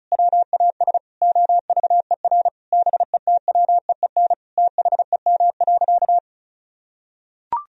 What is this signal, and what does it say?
Morse code, heard as a keyed tone, about 36 wpm, 700 Hz. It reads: WAS OVER BETWEEN THEM.
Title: Sentences from Top 1000 Words - Review Part 1 - 35wpm